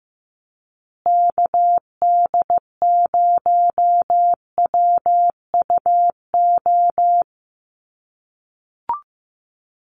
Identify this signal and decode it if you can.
Morse code, heard as a keyed tone, about 15 wpm, 700 Hz. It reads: KD0WUO
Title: USA Call Signs Repeated 3x 15wpm